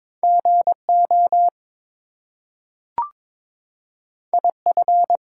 Morse code, heard as a keyed tone, about 22 wpm, 700 Hz. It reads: GO IF
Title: Two Letter Words - 22wpm